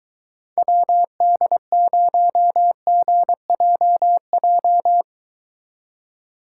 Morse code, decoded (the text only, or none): WD0GJJ